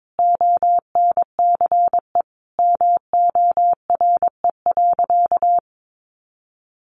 Morse code, heard as a keyed tone, about 22 wpm, 700 Hz. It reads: ONCE MORE.
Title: Sentences from Top 1100 Words 22wpm